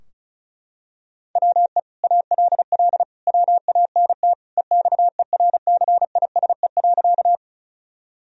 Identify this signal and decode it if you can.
Morse code, heard as a keyed tone, about 36 wpm, 700 Hz. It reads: WE ALL WANT EXERCISE.
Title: Sentences from Top 1000 Words - Review Part 4 - 35wpm